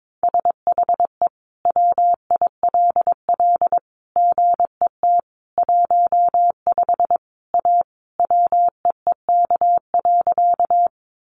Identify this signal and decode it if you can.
Morse code, heard as a keyed tone, about 22 wpm, 700 Hz. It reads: SHE WILL GET 15 A WEEK.
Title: Easy Phrases 22wpm